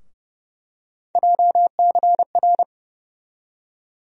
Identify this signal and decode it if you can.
Morse code, heard as a keyed tone, about 30 wpm, 700 Hz. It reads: JCR
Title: USA Call Sign Postfixes - 3 Letters 30wpm